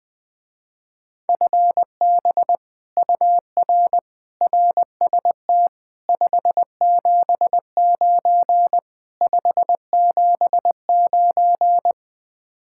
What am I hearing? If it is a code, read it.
Morse code, heard as a keyed tone, about 20 wpm, 700 Hz. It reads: FB UR RST 579 579
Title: Speed-Racing Common QSO Phrases - 17wpm